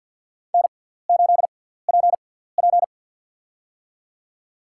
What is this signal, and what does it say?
Morse code, heard as a keyed tone, about 47 wpm, spaced as for 17 wpm, 700 Hz. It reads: N8PP
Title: USA Call Signs - 4 Characters - 6x Character Spacing - 50wpm